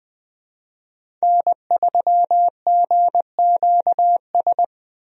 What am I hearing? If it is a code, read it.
Morse code, heard as a keyed tone, about 21 wpm, 700 Hz. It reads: N3GQS
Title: USA Call Signs - 5 Characters - 20wpm